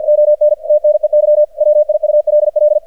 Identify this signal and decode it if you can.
Morse code, heard as a keyed tone, about 50 wpm, 600 Hz, with radio noise. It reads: ON TNEY WIULL
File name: archived / CRSnet